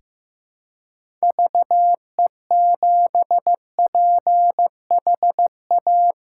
Morse code, heard as a keyed tone, about 15 wpm, 700 Hz. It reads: VE7PHA
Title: K1USN SST QSO Practice - 15wpm